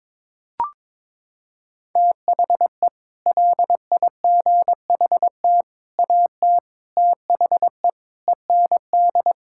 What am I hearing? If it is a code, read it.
Morse code, heard as a keyed tone, about 22 wpm, 700 Hz. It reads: THE LIGHT AT THE END
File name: practice / 22wpm / Sets of 8 Words - 22wpm